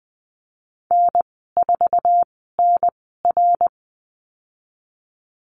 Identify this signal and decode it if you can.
Morse code, heard as a keyed tone, about 20 wpm, spaced as for 14 wpm, 700 Hz. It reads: N4NR
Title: 062 Lesson 11g - Letter h - Call Signs 20wpm